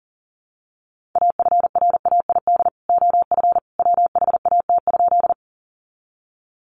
Morse code, heard as a keyed tone, about 40 wpm, 700 Hz. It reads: AFRAID OF WHAT?
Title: Sentences from Top 1500 Words 40wpm